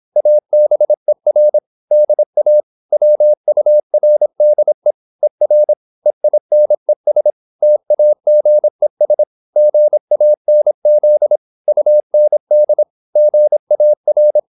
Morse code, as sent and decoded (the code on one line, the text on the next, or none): .- -... .-. / -.. .- / .-- ..- .-. -.. / . .-. / . .. -. ... / - .- --. ... / --. .- -. --.. / ..- -. -.. / --. .- .-.
ABER DA WURDE ER EINES TAGES GANZ UND GAR